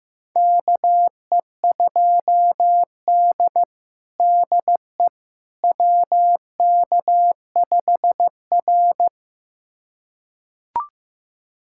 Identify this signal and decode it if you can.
Morse code, heard as a keyed tone, about 15 wpm, 700 Hz. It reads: KE2D DE WK5R
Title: USA Callsign DE Callsign - 4 Characters - 15wpm